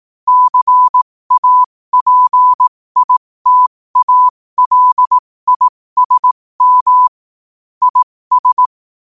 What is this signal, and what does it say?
Morse code, heard as a keyed tone, about 18 wpm, spaced as for 15 wpm, 1000 Hz. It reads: CAPITALISM IS
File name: qotdfiles / 15wpm